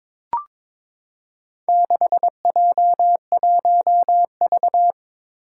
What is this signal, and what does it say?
Morse code, heard as a keyed tone, about 22 wpm, 700 Hz. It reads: 6J1V
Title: Four Challenging Letter-Number Combinations 22wpm